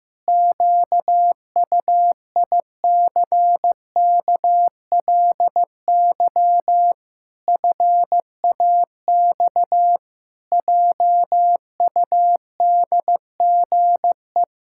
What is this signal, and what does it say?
Morse code, heard as a keyed tone, about 15 wpm, 700 Hz. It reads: QUICKLY FAX JUDGE
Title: Pangram Challenge v2 - 15wpm